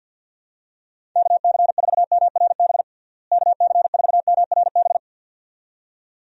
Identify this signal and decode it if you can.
Morse code, heard as a keyed tone, about 49 wpm, 700 Hz. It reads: KC4MRB KC4MRB
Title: USA Call Signs - 6 Characters - Repeated 2x - 50wpm